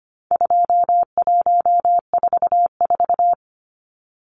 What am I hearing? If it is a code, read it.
Morse code, heard as a keyed tone, about 25 wpm, 700 Hz. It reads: 2144
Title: Numbers - 4-Digit 25wpm